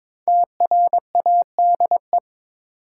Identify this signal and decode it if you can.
Morse code, heard as a keyed tone, about 22 wpm, 700 Hz. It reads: TRADE
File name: 5 Letter Words - First 3 Letters Given - 22wpm